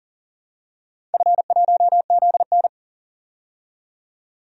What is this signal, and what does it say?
Morse code, heard as a keyed tone, about 40 wpm, 700 Hz. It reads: F1ZN
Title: DXCC-100 Call Signs - 4 Characters - 40wpm